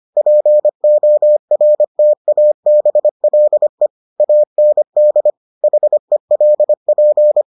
Morse code, heard as a keyed tone, about 25 wpm, 600 Hz. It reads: PORTABLE AND HELP